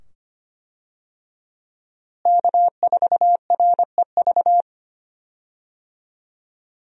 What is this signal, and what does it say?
Morse code, heard as a keyed tone, about 25 wpm, 700 Hz. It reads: K4REV
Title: USA Call Signs - 5 Characters - Repeated 2x - 25wpm